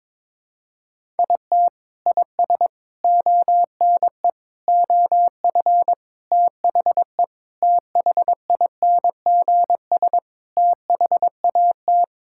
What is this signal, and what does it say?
Morse code, heard as a keyed tone, about 22 wpm, 700 Hz. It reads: IT IS ONE OF THE THINGS THAT
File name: practice / 22wpm / Sentences with 4-Word Phrases - 22wpm